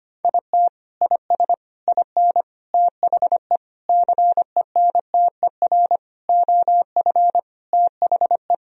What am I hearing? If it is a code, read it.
Morse code, heard as a keyed tone, about 25 wpm, 700 Hz. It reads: IT IS IN THE CENTER OF THE
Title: Sentences from Top 1000 Words - Review Part 2 - 25wpm